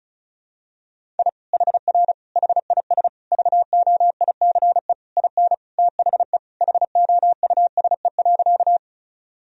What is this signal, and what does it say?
Morse code, heard as a keyed tone, about 34 wpm, 700 Hz. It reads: I HR HIS VOICE IN THE HOUSE.
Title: Sentences from Top 500 Words and Abbreviations 35wpm